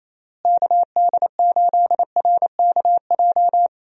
Morse code, heard as a keyed tone, about 27 wpm, 700 Hz. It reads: KD8RKJ
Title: USA Call Signs - 6 Characters - Repeated 3x - 28wpm